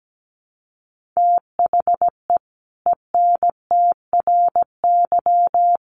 Morse code, heard as a keyed tone, about 17 wpm, 700 Hz. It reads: THE ENTRY
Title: Sentences from Top 1500 Words 17wpm